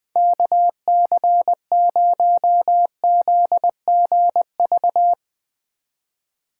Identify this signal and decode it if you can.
Morse code, heard as a keyed tone, about 22 wpm, 700 Hz. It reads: KC0ZGV
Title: Speed Racing - USA 2x3 Call Signs 20wpm